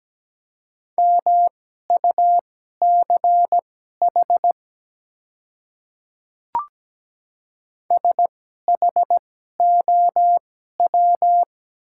Morse code, heard as a keyed tone, about 17 wpm, spaced as for 12 wpm, 700 Hz. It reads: MUCH SHOW
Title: 2x Character Spacing - Top 300 Words - 17wpm